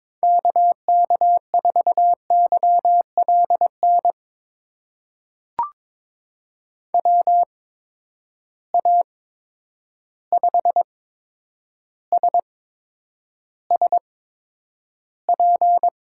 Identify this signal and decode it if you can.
Morse code, heard as a keyed tone, about 22 wpm, spaced as for 9 wpm, 700 Hz. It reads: KK4YLN WA5SSP
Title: USA Call Signs - 6 Characters - 8x Character Spacing - 22wpm